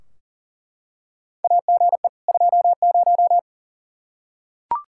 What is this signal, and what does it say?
Morse code, heard as a keyed tone, about 40 wpm, 700 Hz. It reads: AGE 20
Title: Practice QSO - Exchange 3 - More Info - 40wpm